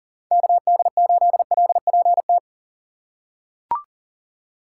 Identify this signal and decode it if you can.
Morse code, heard as a keyed tone, about 40 wpm, 700 Hz. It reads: KD8LPT
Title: USA Call Signs - 6 Characters - 3x Character Spacing - 40wpm